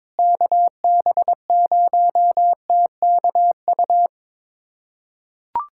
Morse code, heard as a keyed tone, about 22 wpm, 700 Hz. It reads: KB0TKU
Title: USA Callsign DE Callsign - 6 Characters - 22wpm